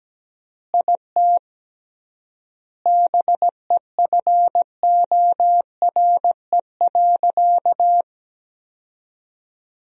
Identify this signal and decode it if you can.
Morse code, heard as a keyed tone, about 17 wpm, 700 Hz. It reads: IT BEFORE.
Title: 3x Word-Spacing - Sentences from Top 400 Words - 17wpm